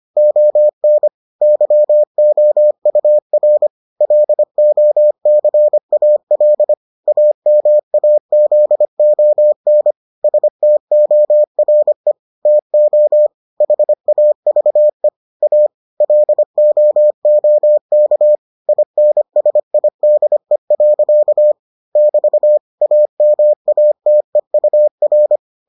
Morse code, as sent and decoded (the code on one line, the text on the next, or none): --- -. / -.-- --- ..- .-. / .-.. --- -.-. .- .-.. / .- -- .- --.. --- -. / ... - --- .-. / - --- / .... .- ...- . / .- / .-.. --- --- -.- / .. -. ... .. -.. .-.-.- / -...- / .- -- .- - . ..- .-.
ON YOUR LOCAL AMAZON STORE TO HAVE A LOOK INSIDE. <BT> AMATEUR